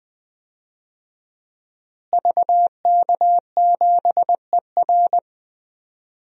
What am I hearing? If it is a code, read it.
Morse code, heard as a keyed tone, about 20 wpm, 700 Hz. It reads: VK7ER
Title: DXCC-100 Call Signs - 5 Characters - Repeated 2x - 20wpm